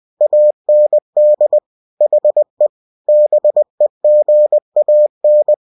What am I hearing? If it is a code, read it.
Morse code, heard as a keyed tone, about 20 wpm, 600 Hz. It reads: AND HE BEGAN
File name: cw-books / winnie-the-pooh / winnie-20wpm-page-0103